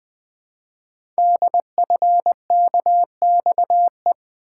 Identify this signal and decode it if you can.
Morse code, heard as a keyed tone, about 20 wpm, 700 Hz. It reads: DFKXE
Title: Code Groups - 5 Letters - 20wpm